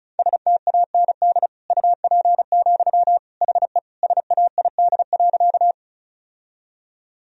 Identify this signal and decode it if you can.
Morse code, heard as a keyed tone, about 34 wpm, 690 Hz. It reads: STAND UP, HE SAID.
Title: Sentences from Top 1000 Words - Review Part 1 - 35wpm